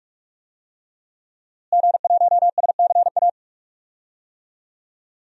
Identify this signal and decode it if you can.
Morse code, heard as a keyed tone, about 46 wpm, 700 Hz. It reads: G1SCA
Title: DXCC-100 Call Signs - 5 Characters - Repeated 3x - 45wpm